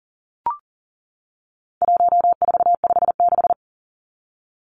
Morse code, heard as a keyed tone, about 40 wpm, 700 Hz. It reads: 1456
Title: Numbers - 4-Digit 40wpm